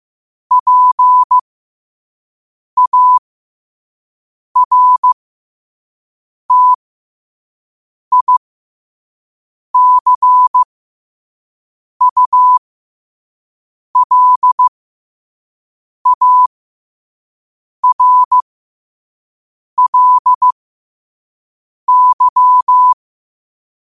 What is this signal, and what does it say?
Morse code, heard as a keyed tone, about 15 wpm, spaced as for 5 wpm, 1000 Hz. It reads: PARTICULARLY
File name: qotdfiles / 5wpm